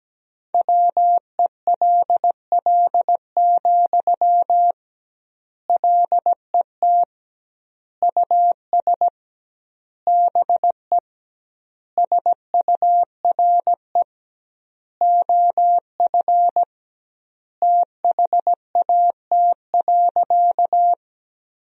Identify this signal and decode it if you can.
Morse code, heard as a keyed tone, about 17 wpm, 700 Hz. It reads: WELL, LET US BE SURE OF THAT.
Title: 2x Word-Spacing - Sentences from Top 300 Words - 17wpm